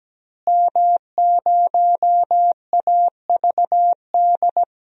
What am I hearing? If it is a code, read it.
Morse code, heard as a keyed tone, about 17 wpm, 700 Hz. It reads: M0AVD
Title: DXCC-100 Call Signs - 5 Characters - Repeated 2x - 17wpm